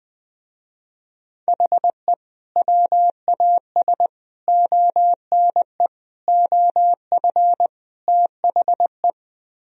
Morse code, heard as a keyed tone, about 20 wpm, 700 Hz. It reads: HE WAS ONE OF THE